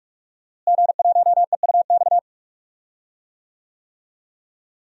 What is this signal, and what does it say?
Morse code, heard as a keyed tone, about 46 wpm, 700 Hz. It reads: G1EUX
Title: DXCC-100 Call Signs - 5 Characters - Repeated 2x - 45wpm